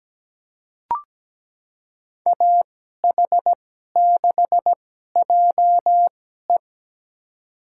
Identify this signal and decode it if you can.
Morse code, heard as a keyed tone, about 17 wpm, spaced as for 12 wpm, 700 Hz. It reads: AH6JE
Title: USA Call Signs - 5 Characters - 2x Character Spacing - 17wpm